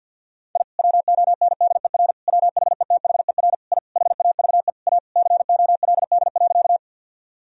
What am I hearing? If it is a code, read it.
Morse code, heard as a keyed tone, about 46 wpm, 700 Hz. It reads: I WONDER WHETHER I HAVE A COLD.
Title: Sentences from Top 1000 Words - Review Part 3 - 50wpm